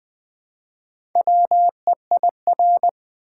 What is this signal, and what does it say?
Morse code, heard as a keyed tone, about 20 wpm, 700 Hz. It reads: WEIR
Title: Build A Word - 1 Letter At A Time - 5-Letter Words - 20wpm